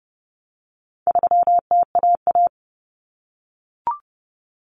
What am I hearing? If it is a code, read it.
Morse code, heard as a keyed tone, about 30 wpm, 700 Hz. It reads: 3TAA
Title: Common Cut Numbers - 4-Digit 30wpm